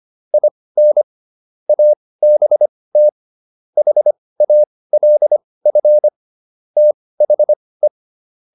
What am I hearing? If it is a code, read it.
Morse code, heard as a keyed tone, about 25 wpm, spaced as for 18 wpm, 600 Hz. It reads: IN ABT HALF THE